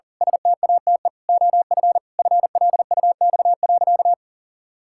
Morse code, heard as a keyed tone, about 40 wpm, 700 Hz. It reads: STATE OF FLUX.
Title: Pangram Challenge v2 - 40wpm